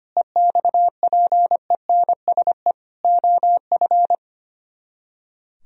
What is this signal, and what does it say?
Morse code, heard as a keyed tone, about 32 wpm, 700 Hz. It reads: EXPENSE OF